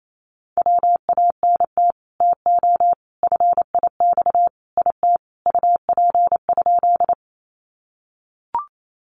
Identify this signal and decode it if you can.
Morse code, heard as a keyed tone, about 28 wpm, 700 Hz. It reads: WANT TO FIX IT UP?
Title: Sentences from Top 1400 Words 28wpm